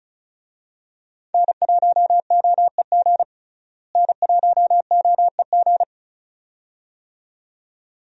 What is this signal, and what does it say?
Morse code, heard as a keyed tone, about 34 wpm, 700 Hz. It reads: N1OEG N1OEG
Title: USA Call Signs - 5 Characters - Repeated 2x - 35wpm